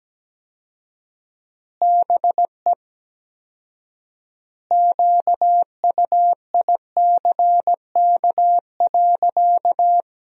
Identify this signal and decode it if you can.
Morse code, heard as a keyed tone, about 17 wpm, 700 Hz. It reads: BE QUICK.
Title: 4x Word-Spacing - Sentences from Top 500 Words - 17wpm